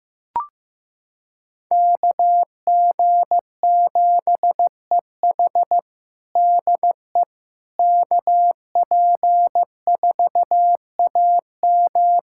Morse code, heard as a keyed tone, about 15 wpm, 700 Hz. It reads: KG7EH DE KP4AM
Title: USA Callsign DE Callsign - 5 Characters - 15wpm